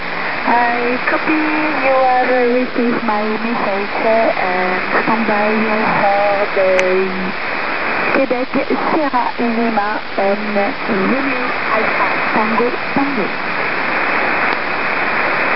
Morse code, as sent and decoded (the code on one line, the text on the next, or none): none